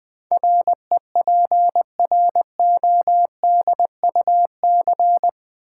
Morse code, heard as a keyed tone, about 20 wpm, 700 Hz. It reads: REPRODUC
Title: Build A Word - 4 Letters At A Time - 12-Letter Words - 20wpm